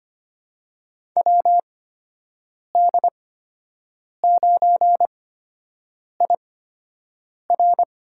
Morse code, heard as a keyed tone, about 25 wpm, spaced as for 7 wpm, 700 Hz. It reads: WD9IR
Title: USA Call Signs - 8x Character Spacing - 25wpm